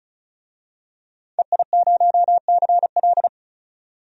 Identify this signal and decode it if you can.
Morse code, heard as a keyed tone, about 36 wpm, 700 Hz. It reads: EI0CL
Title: DXCC-100 Call Signs - 5 Characters - 35wpm